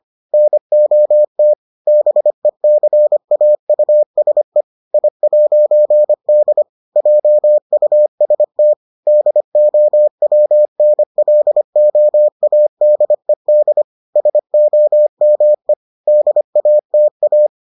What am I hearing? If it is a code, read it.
Morse code, heard as a keyed tone, about 25 wpm, 600 Hz. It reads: NOT BECAUSE I'D JUST DOWNLOADED SOME DATA